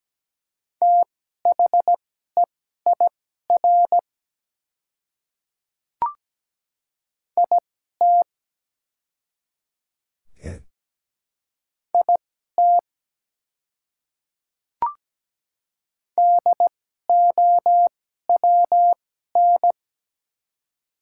Morse code, heard as a keyed tone, about 17 wpm, spaced as for 12 wpm, 700 Hz. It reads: THEIR IT IT DOWN
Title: 2x Character Spacing - Top 200 Words - 17wpm